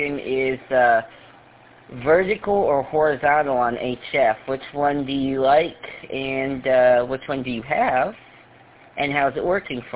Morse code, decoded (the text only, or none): none